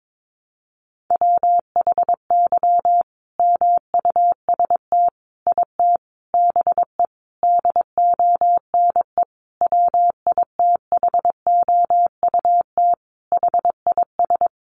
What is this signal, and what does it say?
Morse code, heard as a keyed tone, about 22 wpm, 700 Hz. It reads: WHY MUST IT BE DONE WITHOUT HIS
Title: Sentences from Top 600 Words 22wpm